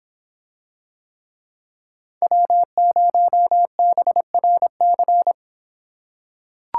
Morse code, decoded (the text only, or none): W0BRC